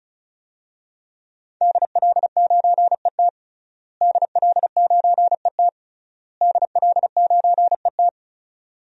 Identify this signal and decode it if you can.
Morse code, heard as a keyed tone, about 36 wpm, 700 Hz. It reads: DL9ET DL9ET DL9ET
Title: DXCC-100 Call Signs - 5 Characters - Repeated 3x - 35wpm